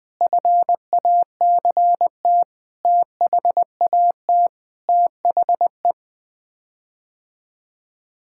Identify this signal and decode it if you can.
Morse code, heard as a keyed tone, about 20 wpm, 700 Hz. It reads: FACT THAT THE
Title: Sets of 8 Words - 20wpm